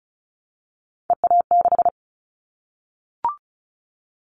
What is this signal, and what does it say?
Morse code, heard as a keyed tone, about 36 wpm, 700 Hz. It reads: EA6
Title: Callsign Prefixes for Top 100 DXCC - 35wpm